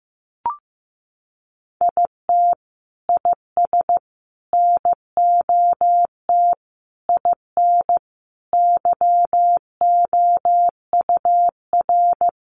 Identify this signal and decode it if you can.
Morse code, heard as a keyed tone, about 14 wpm, 700 Hz. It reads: IT IS NOT IN YOUR
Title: Sentences from Top 1400 Words 15wpm